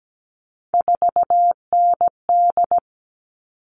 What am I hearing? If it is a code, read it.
Morse code, heard as a keyed tone, about 17 wpm, 700 Hz. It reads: 4ND
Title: All Cut Numbers - 3-Digit 17wpm